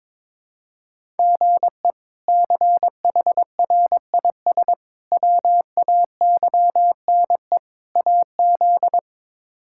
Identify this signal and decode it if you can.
Morse code, heard as a keyed tone, about 21 wpm, 700 Hz. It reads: GE CHRIS WAYNE AZ
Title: K1USN SST QSO Practice - 22wpm